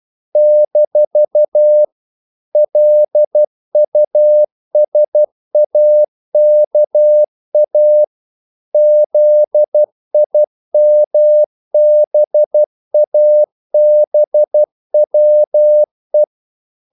Morse code, heard as a keyed tone, about 12 wpm, 600 Hz. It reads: - LUSAKA ZIMBABWE
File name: CW / Practice / NationalCapitals12